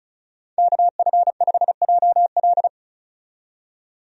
Morse code, heard as a keyed tone, about 36 wpm, 700 Hz. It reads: KF5JL
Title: USA Call Signs - 5 Characters - 35wpm